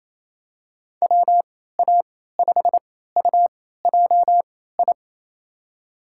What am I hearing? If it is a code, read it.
Morse code, heard as a keyed tone, about 28 wpm, spaced as for 16 wpm, 700 Hz. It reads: WA5UJI
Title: USA Call Signs - 6 Characters - 3x Character Spacing - 28wpm